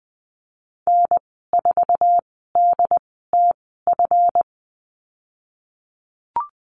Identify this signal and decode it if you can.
Morse code, heard as a keyed tone, about 20 wpm, spaced as for 15 wpm, 700 Hz. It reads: N4DTF
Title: 161 Lesson 26g - Letter g - Call Signs 20wpm